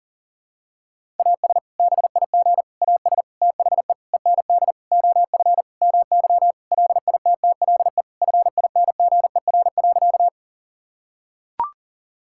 Morse code, heard as a keyed tone, about 40 wpm, 700 Hz. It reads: AS BIG AS THE END OF MY LITTLE FINGER.
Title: Sentences from Top 1600 Words 40wpm